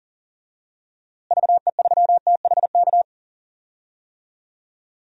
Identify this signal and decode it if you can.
Morse code, heard as a keyed tone, about 37 wpm, 700 Hz. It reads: VE3THK
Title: DXCC-100 Call Signs - 6 Characters - Repeated 3x - 40wpm